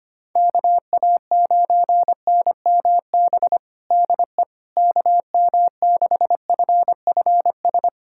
Morse code, heard as a keyed tone, about 25 wpm, 700 Hz. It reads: KA9NMB DE KM6FFS
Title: USA Callsign DE Callsign 25wpm